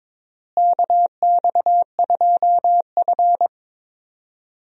Morse code, heard as a keyed tone, about 22 wpm, 700 Hz. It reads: KX2F
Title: USA Call Signs - 4 Characters - Repeated 2x - 22wpm